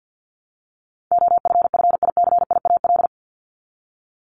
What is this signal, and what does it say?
Morse code, heard as a keyed tone, about 50 wpm, 700 Hz. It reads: OFFICIAL
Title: Top 751-1000 Words 50wpm